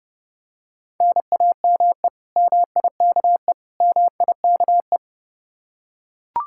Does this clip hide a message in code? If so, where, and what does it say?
Yes: Morse code from 1.0 to 5.0 s, heard as a keyed tone, about 30 wpm, 700 Hz. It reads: NAME MIKE MIKE